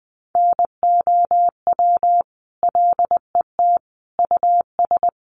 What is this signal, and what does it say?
Morse code, heard as a keyed tone, about 20 wpm, 700 Hz. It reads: NOW LET US
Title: Sentences from Top 1200 Words 20wpm